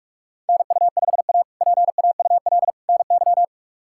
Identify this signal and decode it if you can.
Morse code, heard as a keyed tone, about 46 wpm, 700 Hz. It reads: NU5A PAUL NY
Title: K1USN SST QSO Practice - 45wpm